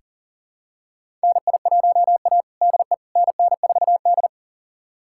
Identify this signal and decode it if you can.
Morse code, heard as a keyed tone, about 40 wpm, 700 Hz. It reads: NI1A DE NN4D
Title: USA Callsign DE Callsign - 4 Characters - 40wpm